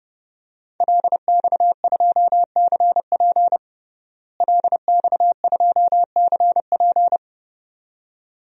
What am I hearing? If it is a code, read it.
Morse code, heard as a keyed tone, about 32 wpm, 700 Hz. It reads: LX2CP LX2CP
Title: DXCC-100 Call Signs - 5 Characters - Repeated 2x - 30wpm